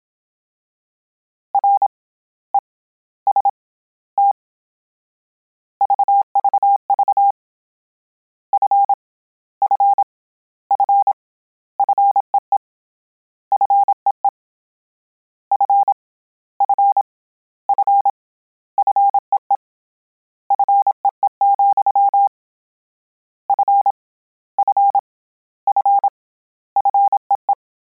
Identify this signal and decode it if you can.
Morse code, heard as a keyed tone, about 27 wpm, 800 Hz. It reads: R E S T VVV F F F FEE FEE F F F FEE FEE, F F F FEE
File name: Combinations / Combinations-mp3s / Combinations@25